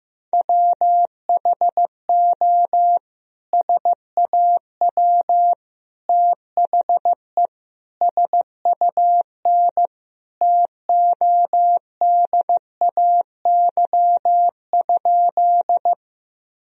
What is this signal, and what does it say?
Morse code, heard as a keyed tone, about 15 wpm, 700 Hz. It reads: WHO SAW THE SUN TODAY?
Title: Sentences from Top 1000 Words - Review Part 1 - 15wpm